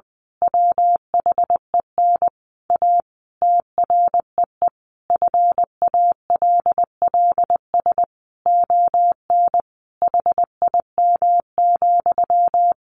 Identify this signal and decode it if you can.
Morse code, heard as a keyed tone, about 20 wpm, 700 Hz. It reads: WHEN A TREE FALLS ON HIM,
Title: A Sound Of Thunder 20wpm